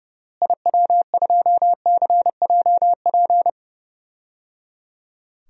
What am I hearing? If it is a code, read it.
Morse code, heard as a keyed tone, about 30 wpm, 700 Hz. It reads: IW2CJP
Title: DXCC-100 Call Signs - 6 Characters - Repeated 3x - 30wpm